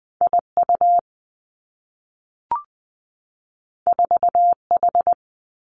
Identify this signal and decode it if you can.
Morse code, heard as a keyed tone, about 20 wpm, 700 Hz. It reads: I U 4 H
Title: Two Challenging Letter-Number Combinations 20wpm